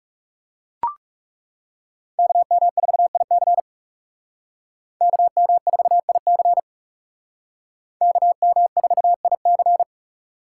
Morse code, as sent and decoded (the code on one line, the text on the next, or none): -.- -- ....- .. -.-. / -.- -- ....- .. -.-. / -.- -- ....- .. -.-.
KM4IC KM4IC KM4IC